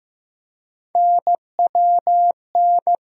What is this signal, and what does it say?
Morse code, heard as a keyed tone, about 15 wpm, 700 Hz. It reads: NWN
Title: USA Call Sign Postfixes - 3 Letters 15wpm